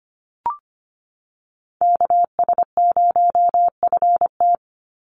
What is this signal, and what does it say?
Morse code, heard as a keyed tone, about 25 wpm, 700 Hz. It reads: KS0FT
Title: USA Call Signs 25wpm